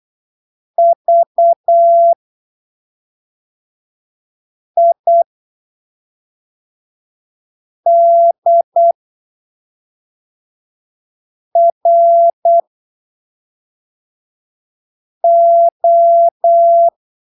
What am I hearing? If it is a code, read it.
Morse code, heard as a keyed tone, about 8 wpm, spaced as for 3 wpm, 680 Hz. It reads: VIDRO
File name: arquivos / texto / texto 11 08ppm